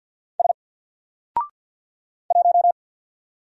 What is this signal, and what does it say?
Morse code, heard as a keyed tone, about 48 wpm, 700 Hz. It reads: S 1
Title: Single Letter-Number Alt 50wpm